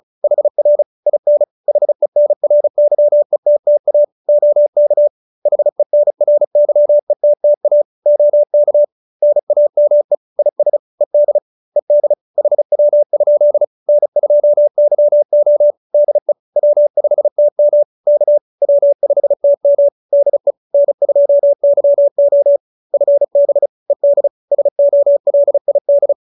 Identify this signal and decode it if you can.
Morse code, heard as a keyed tone, about 34 wpm, 590 Hz. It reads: HR IN HENRYETTA OK HENRYETTA OK NAME IS ED ED HW? N2YO DE W5TM K W5TM DE N2YO FB ED SOLID